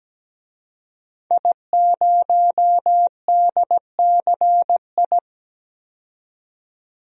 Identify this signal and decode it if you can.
Morse code, heard as a keyed tone, about 17 wpm, 700 Hz. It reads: I0DCI
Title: DXCC-100 Call Signs - 5 Characters - 17wpm